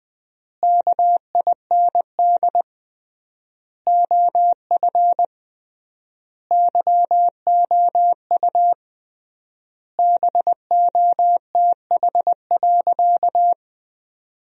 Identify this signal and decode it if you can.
Morse code, heard as a keyed tone, about 21 wpm, 700 Hz. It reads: KIND OF YOU BOTH.
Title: 3x Word-Spacing - Sentences from Top 300 Words - 20wpm